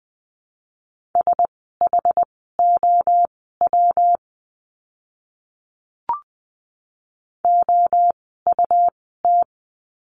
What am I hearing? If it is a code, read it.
Morse code, heard as a keyed tone, about 20 wpm, spaced as for 14 wpm, 700 Hz. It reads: SHOW OUT
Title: 192 Lesson 31f - Letter v - Words 20wpm